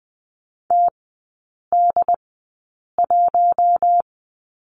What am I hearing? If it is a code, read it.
Morse code, heard as a keyed tone, about 20 wpm, spaced as for 8 wpm, 700 Hz. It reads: TD1
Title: 114 Lesson 19e - Letter w - Three Characters 20wpm